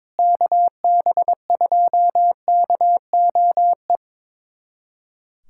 Morse code, heard as a keyed tone, about 22 wpm, 700 Hz. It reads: KB2KOE
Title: USA Callsign DE Callsign - 6 Characters - 22wpm